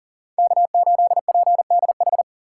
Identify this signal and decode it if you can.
Morse code, heard as a keyed tone, about 40 wpm, 700 Hz. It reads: K8PDH
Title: USA Call Signs - 5 Characters - 3x Character Spacing - 40wpm